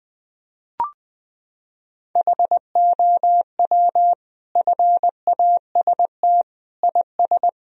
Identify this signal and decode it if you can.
Morse code, heard as a keyed tone, about 20 wpm, 700 Hz. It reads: HOW FAST IS
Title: Sentences from Top 1000 Words - Review Part 4 - 20wpm